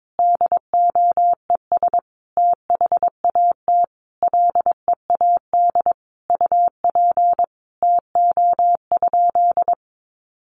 Morse code, heard as a keyed tone, about 22 wpm, 700 Hz. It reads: DOES THAT LEAD UP TO?
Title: Sentences from Top 500 Words 22wpm